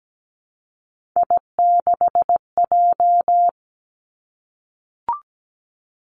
Morse code, heard as a keyed tone, about 17 wpm, 700 Hz. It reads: I6J
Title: Three Challenging Letter-Number Combinations 17wpm